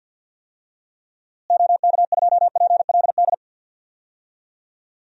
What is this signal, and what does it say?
Morse code, heard as a keyed tone, about 46 wpm, 700 Hz. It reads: OK2PLD
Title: DXCC-100 Call Signs - 6 Characters - Repeated 3x - 50wpm